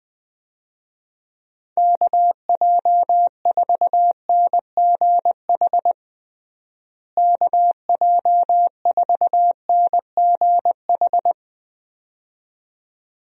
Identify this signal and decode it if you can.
Morse code, heard as a keyed tone, about 21 wpm, 700 Hz. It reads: KJ4NGH KJ4NGH
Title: USA Call Signs - 6 Characters - Repeated 2x - 20wpm